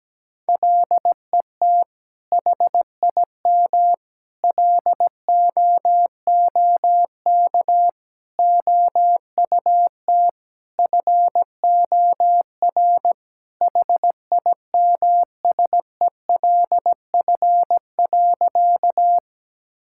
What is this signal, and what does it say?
Morse code, heard as a keyed tone, about 17 wpm, 700 Hz. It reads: LET HIM LOOK OUT FOR HIMSELF.
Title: Sentences from Top 1600 Words 17wpm